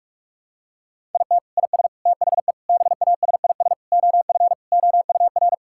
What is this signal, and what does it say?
Morse code, heard as a keyed tone, about 46 wpm, 700 Hz. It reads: IT IS THE BASIS OF OUR
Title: Sentences from Top 1000 Words - Review Part 5 - 45wpm